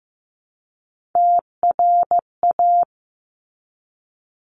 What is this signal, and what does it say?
Morse code, heard as a keyed tone, about 15 wpm, 700 Hz. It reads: TRA
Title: Top Three Letter Combinations in Words 15wpm